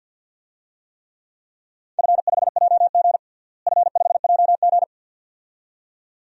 Morse code, heard as a keyed tone, about 53 wpm, 700 Hz. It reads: F5JG F5JG
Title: DXCC-100 Call Signs - 4 Characters - Repeated 2x - 50wpm